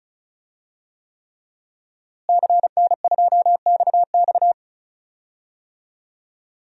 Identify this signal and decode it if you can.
Morse code, heard as a keyed tone, about 34 wpm, 700 Hz. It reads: CN2XX
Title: DXCC-100 Call Signs - 5 Characters - Repeated 3x - 35wpm